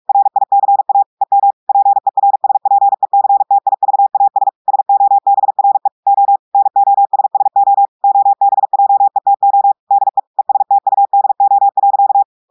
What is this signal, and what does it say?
Morse code, heard as a keyed tone, about 46 wpm, 800 Hz. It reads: RICA EM PERSPECTIVAS SOBRE O NOSSO OBJETO DE ESTUDO.
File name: audios / portugais / portugais texte en clair 1 vitesse 45 mots minutes